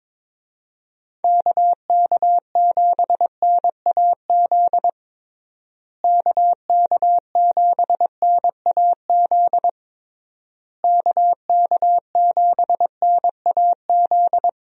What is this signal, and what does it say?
Morse code, heard as a keyed tone, about 22 wpm, 700 Hz. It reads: KK7NAZ KK7NAZ KK7NAZ
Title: USA Call Signs - 6 Characters - Repeated 3x - 22wpm